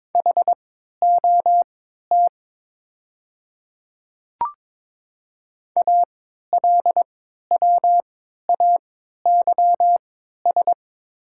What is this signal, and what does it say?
Morse code, heard as a keyed tone, about 22 wpm, spaced as for 12 wpm, 700 Hz. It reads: HOT ALWAYS
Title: 3x Character Spacing - Top 500 Words - 22wpm